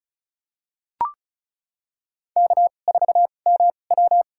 Morse code, heard as a keyed tone, about 34 wpm, spaced as for 25 wpm, 700 Hz. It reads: K4MW
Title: USA Call Signs - 4 Characters - 2x Character Spacing - 35wpm